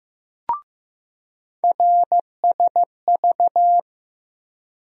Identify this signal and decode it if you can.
Morse code, heard as a keyed tone, about 15 wpm, 700 Hz. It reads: RSV
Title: USA Call Sign Postfixes - 3 Letters 15wpm